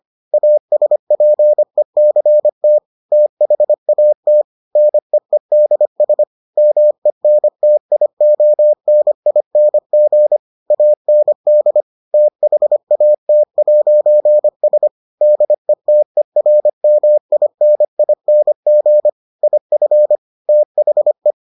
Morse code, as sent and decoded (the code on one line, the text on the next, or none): .- ... .--. -.-. - / - .... .- - / -. . -.. ... / -- . -. - .. --- -. .. -. --. / .- -. -.. / - .... .- - .----. ... / -.. - . .-. -- .. -. .. -. --. / .. ..-. / - .... .
ASPECT THAT NEEDS MENTIONING AND THAT'S DETERMINING IF THE